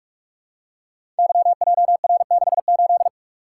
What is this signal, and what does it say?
Morse code, heard as a keyed tone, about 44 wpm, 700 Hz. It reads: YJR68
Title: Code Groups - 5 Letter:Numbers - 45wpm